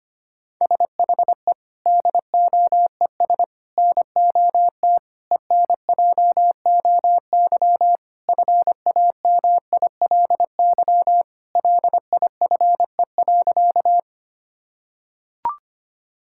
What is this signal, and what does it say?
Morse code, heard as a keyed tone, about 25 wpm, 700 Hz. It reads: SHE DOES NOT ENJOY FAMILY LIFE.
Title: Sentences from Top 1000 Words - Review Part 5 - 25wpm